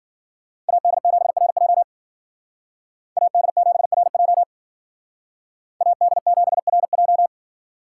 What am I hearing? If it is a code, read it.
Morse code, heard as a keyed tone, about 52 wpm, 700 Hz. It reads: AD7RJ AD7RJ AD7RJ